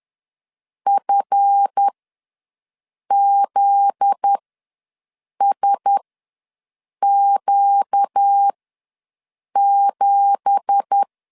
Morse code, heard as a keyed tone, about 10 wpm, spaced as for 6 wpm, 800 Hz. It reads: FZSQ7
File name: LEZIONI / 9 2867 ws 10wpm 4215